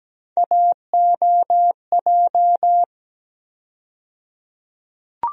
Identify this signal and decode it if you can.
Morse code, heard as a keyed tone, about 17 wpm, 700 Hz. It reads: AOJ
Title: USA Call Sign Postfixes - 3 Letters 17wpm